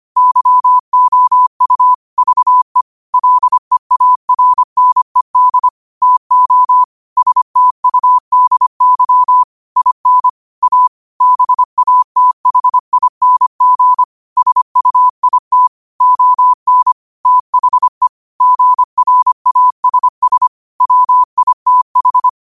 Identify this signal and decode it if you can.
Morse code, heard as a keyed tone, about 25 wpm, 1000 Hz. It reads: YOU VE LEARNED TO STUDY IN A BATHING SUIT ON THE GRASS WITH
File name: qotdfiles / 25wpm